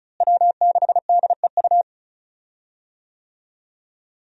Morse code, heard as a keyed tone, about 35 wpm, 700 Hz. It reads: W6DEU
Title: USA Call Signs - 5 Characters - Repeated 3x - 35wpm